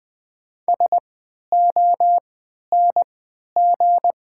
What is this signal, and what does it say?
Morse code, heard as a keyed tone, about 20 wpm, spaced as for 11 wpm, 690 Hz. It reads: SONG
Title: 3x Character Spacing - Top 500 Words - 20wpm